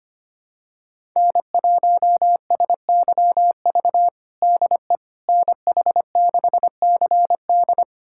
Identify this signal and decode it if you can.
Morse code, heard as a keyed tone, about 25 wpm, 700 Hz. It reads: N1SYV DE NH6CD
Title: USA Callsign DE Callsign - 5 Characters - 25wpm